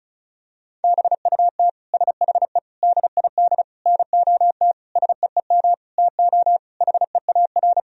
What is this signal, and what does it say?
Morse code, heard as a keyed tone, about 36 wpm, 700 Hz. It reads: BUT SHE DID NOT SEEM TO HEAR
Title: Sentences from Top 1000 Words - Review Part 1 - 35wpm